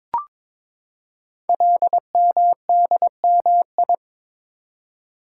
Morse code, heard as a keyed tone, about 22 wpm, 700 Hz. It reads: LMDMI
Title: Code Groups - 5 Letters - 22wpm